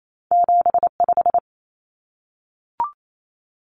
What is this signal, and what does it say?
Morse code, heard as a keyed tone, about 28 wpm, 700 Hz. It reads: 7 5
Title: Common Cut Numbers - 2-Digit 28wpm